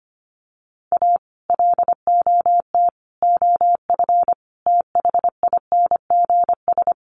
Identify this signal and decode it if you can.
Morse code, heard as a keyed tone, about 25 wpm, 700 Hz. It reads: A LOT OF THINGS
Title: Sets of 5 Words 25wpm